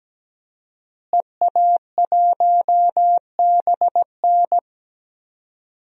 Morse code, heard as a keyed tone, about 17 wpm, 700 Hz. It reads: EA1BN
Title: DXCC-100 Call Signs - 5 Characters - Repeated 2x - 17wpm